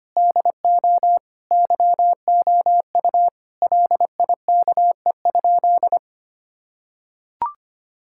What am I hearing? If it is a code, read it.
Morse code, heard as a keyed tone, about 25 wpm, 700 Hz. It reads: DO YOU LIKE?
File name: practice / 25wpm / Sentences with 3-Word Phrases - 25wpm